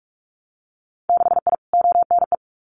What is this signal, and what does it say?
Morse code, heard as a keyed tone, about 46 wpm, 690 Hz. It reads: 6I ONE
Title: Winter Field Day Exchanges Encore 45wpm